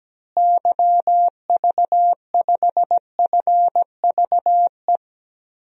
Morse code, heard as a keyed tone, about 17 wpm, 700 Hz. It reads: YV5FVE
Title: DXCC-100 Call Signs - 6 Characters - Repeated 3x - 17wpm